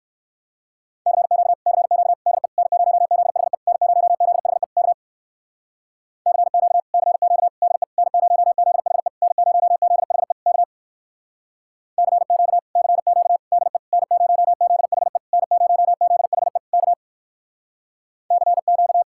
Morse code, heard as a keyed tone, about 63 wpm, 700 Hz. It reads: CQ CQ DE N0ZHE N0ZHE K CQ CQ DE N0ZHE N0ZHE K CQ CQ DE N0ZHE N0ZHE K CQ